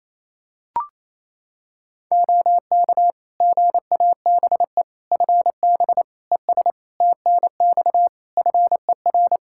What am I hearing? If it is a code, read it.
Morse code, heard as a keyed tone, about 27 wpm, 700 Hz. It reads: OK GABE FB ES TNX FER
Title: Practice QSO - Exchange 2 - Info - 28wpm